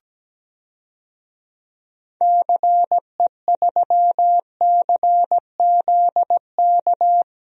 Morse code, heard as a keyed tone, about 17 wpm, 700 Hz. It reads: CE3CZK
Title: DXCC-100 Call Signs - 6 Characters - 17wpm